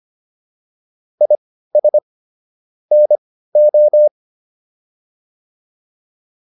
Morse code, heard as a keyed tone, about 25 wpm, spaced as for 15 wpm, 600 Hz. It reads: IS NO